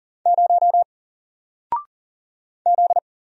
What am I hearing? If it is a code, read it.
Morse code, heard as a keyed tone, about 40 wpm, 700 Hz. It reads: 0 Z
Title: Single Letter-Number 40wpm